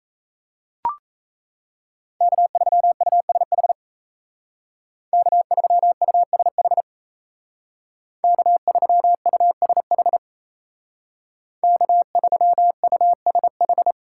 Morse code, heard as a keyed tone, about 36 wpm, 700 Hz. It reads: K3USH K3USH K3USH K3USH